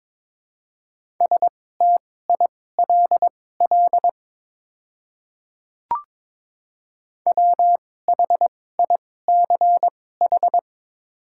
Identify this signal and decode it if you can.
Morse code, heard as a keyed tone, about 22 wpm, spaced as for 16 wpm, 700 Hz. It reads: STILL WHICH
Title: 2x Character Spacing - Top 500 Words - 22wpm